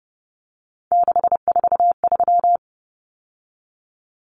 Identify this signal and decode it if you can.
Morse code, heard as a keyed tone, about 30 wpm, 700 Hz. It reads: 643
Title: Common Cut Numbers - 3-Digit 30wpm